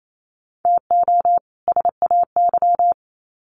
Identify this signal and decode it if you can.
Morse code, heard as a keyed tone, about 28 wpm, 700 Hz. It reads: TO SAY
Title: Sets of 2 Words 28wpm